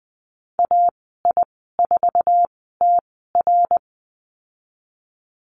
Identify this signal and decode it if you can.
Morse code, heard as a keyed tone, about 20 wpm, spaced as for 14 wpm, 700 Hz. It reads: AI4TR
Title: 069 Lesson 12g - Letter d - Call Signs 20wpm